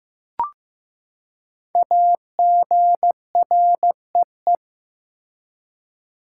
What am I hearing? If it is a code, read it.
Morse code, heard as a keyed tone, about 15 wpm, 700 Hz. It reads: AGREE
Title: Five Letter Words - 15wpm